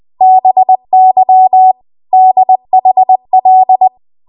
Morse code, heard as a keyed tone, about 20 wpm, 750 Hz. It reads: BY DHL
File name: Archive / 20 WPM / 20WPM